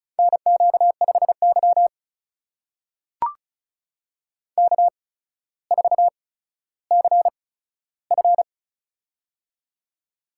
Morse code, heard as a keyed tone, about 36 wpm, spaced as for 15 wpm, 700 Hz. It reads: NQ5Y K4CF
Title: USA Call Signs - 4 Characters - 8x Character Spacing - 35wpm